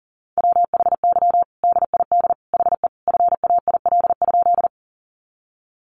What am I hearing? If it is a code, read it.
Morse code, heard as a keyed tone, about 40 wpm, 700 Hz. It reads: WHY DID HE FAIL?
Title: Sentences from Top 1100 Words 40wpm